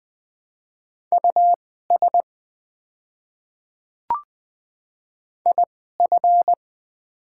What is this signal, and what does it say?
Morse code, heard as a keyed tone, about 20 wpm, spaced as for 15 wpm, 700 Hz. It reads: US IF
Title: 2x Character Spacing - Top 500 Words - 20wpm